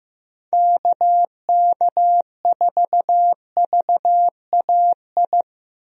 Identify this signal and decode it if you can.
Morse code, heard as a keyed tone, about 15 wpm, 700 Hz. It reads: KK4VAI
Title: USA Call Signs - 6 Characters - Repeated 2x - 15wpm